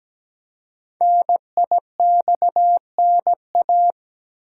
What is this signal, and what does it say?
Morse code, heard as a keyed tone, about 17 wpm, 700 Hz. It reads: NIXNA
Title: Code Groups - 5 Letters - 17wpm